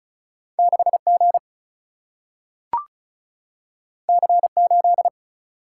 Morse code, heard as a keyed tone, about 36 wpm, 700 Hz. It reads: 6 G C 8
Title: Single-Letter Single-Number 35wpm